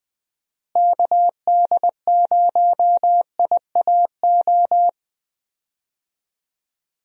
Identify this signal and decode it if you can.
Morse code, heard as a keyed tone, about 20 wpm, 700 Hz. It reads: KD0IAO
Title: USA Call Signs - 6 Characters - Repeated 3x - 20wpm